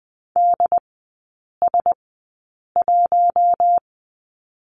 Morse code, heard as a keyed tone, about 19 wpm, spaced as for 8 wpm, 700 Hz. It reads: DS1
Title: 067 Lesson 12e - Letter d - Three Characters 20wpm